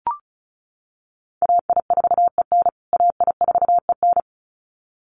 Morse code, heard as a keyed tone, about 34 wpm, 700 Hz. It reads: AI4EN AI4EN
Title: USA Call Signs Repeated 2x 35wpm